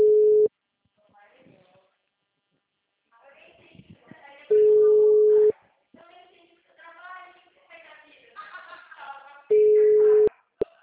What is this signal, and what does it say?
Ringback tone (Europe, 425 Hz), with speech.
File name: out- -2020- - - .24